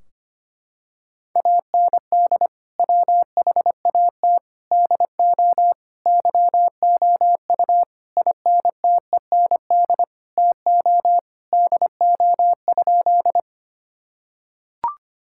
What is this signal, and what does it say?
Morse code, heard as a keyed tone, about 25 wpm, 700 Hz. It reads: AND WHAT DO YOU INTEND TO DO?
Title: Sentences from Top 1600 Words 25wpm